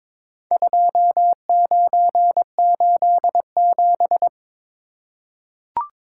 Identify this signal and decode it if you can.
Morse code, heard as a keyed tone, about 23 wpm, 700 Hz. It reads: 2987
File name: practice / 22wpm / Four Letter-Number Combinations - 22wpm